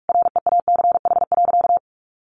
Morse code, heard as a keyed tone, about 44 wpm, 700 Hz. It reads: REACH.
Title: Sentences from Top 1400 Words 45wpm